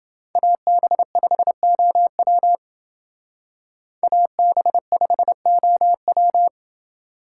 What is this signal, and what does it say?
Morse code, heard as a keyed tone, about 29 wpm, 700 Hz. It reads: AB5OW AB5OW